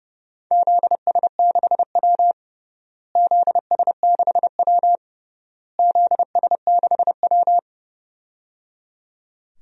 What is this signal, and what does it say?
Morse code, heard as a keyed tone, about 30 wpm, 700 Hz. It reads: ZS6W ZS6W ZS6W
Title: DXCC-100 Call Signs - 4 Characters - Repeated 3x - 30wpm